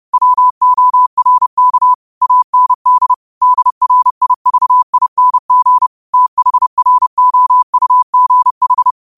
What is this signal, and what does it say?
Morse code, heard as a keyed tone, about 31 wpm, 1000 Hz. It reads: WORK AND DRIVING THROUGH
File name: qotdfiles / 30wpm